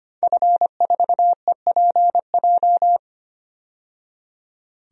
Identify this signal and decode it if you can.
Morse code, heard as a keyed tone, about 25 wpm, 700 Hz. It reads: F4EPJ
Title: DXCC-100 Call Signs - 5 Characters - 25wpm